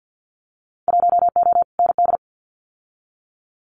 Morse code, heard as a keyed tone, about 50 wpm, 700 Hz. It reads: 1O ND
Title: Winter Field Day Exchanges 50wpm